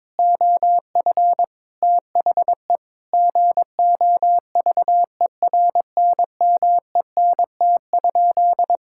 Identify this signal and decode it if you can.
Morse code, heard as a keyed tone, about 22 wpm, 700 Hz. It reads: OF THE GOVERNMENT?
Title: Sentences from Top 1000 Words - Review Part 5 - 22wpm